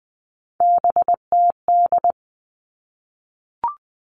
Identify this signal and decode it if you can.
Morse code, heard as a keyed tone, about 20 wpm, 700 Hz. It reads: BTD
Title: All Cut Numbers - 3-Digit 20wpm